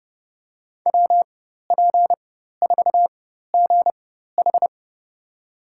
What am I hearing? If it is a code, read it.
Morse code, heard as a keyed tone, about 30 wpm, spaced as for 14 wpm, 700 Hz. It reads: WP4GH
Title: USA Call Signs - 5 Characters - 4x Character Spacing - 30wpm